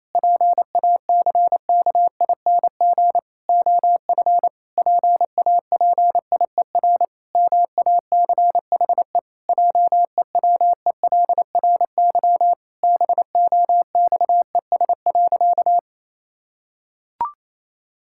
Morse code, heard as a keyed tone, about 28 wpm, 700 Hz. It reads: PACKING OF PAPIER MACHE JEWELRY BOXES.
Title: Pangram Challenge v2 - 28wpm